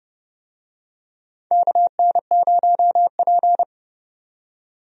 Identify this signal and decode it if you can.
Morse code, heard as a keyed tone, about 30 wpm, 700 Hz. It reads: KN0P
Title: USA Call Signs - 4 Characters - 30wpm